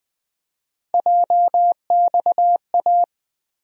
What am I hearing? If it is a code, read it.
Morse code, heard as a keyed tone, about 20 wpm, 700 Hz. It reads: JXA
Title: USA Call Sign Postfixes - 3 Letters 20wpm